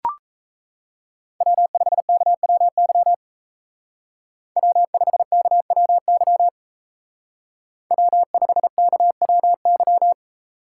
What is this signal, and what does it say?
Morse code, heard as a keyed tone, about 34 wpm, 700 Hz. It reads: W5KWY W5KWY W5KWY
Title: Speed Racing - USA Call Signs 28wpm